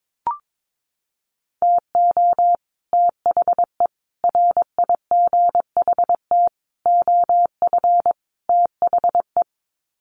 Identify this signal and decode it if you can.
Morse code, heard as a keyed tone, about 22 wpm, 700 Hz. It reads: TO THE RIGHT OF THE
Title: Sets of 5 Words Encore 22wpm